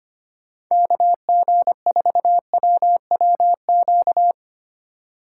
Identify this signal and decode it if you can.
Morse code, heard as a keyed tone, about 25 wpm, 700 Hz. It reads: KG4WWQ
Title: USA Call Signs - 6 Characters - 3x Character Spacing - 25wpm